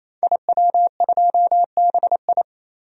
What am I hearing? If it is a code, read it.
Morse code, heard as a keyed tone, about 28 wpm, 700 Hz. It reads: IW2BI
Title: DXCC-100 Call Signs - 5 Characters - 28wpm